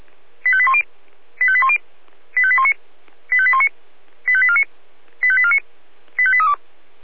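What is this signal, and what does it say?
Five-tone selective call (ZVEI), heard as quick runs of five tones, tones of about 70 ms. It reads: 85810 85810 85819 85819 85849 85849 85832
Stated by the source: ZVEI five-tone selective call received on an SDR.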